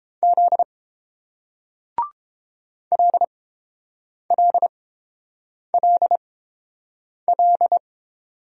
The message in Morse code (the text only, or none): Z LLLL